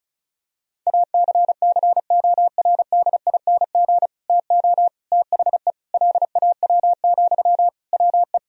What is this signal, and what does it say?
Morse code, heard as a keyed tone, about 36 wpm, 700 Hz. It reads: ACCORDING TO THE LAW, WE
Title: Sentences from Top 1000 Words - Review Part 4 - 35wpm